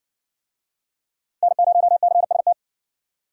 Morse code, heard as a keyed tone, about 55 wpm, 700 Hz. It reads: N0CST